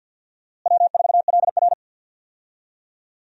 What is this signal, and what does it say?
Morse code, heard as a keyed tone, about 52 wpm, 700 Hz. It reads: W4LR
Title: USA Call Signs - 4 Characters - Repeated 2x - 50wpm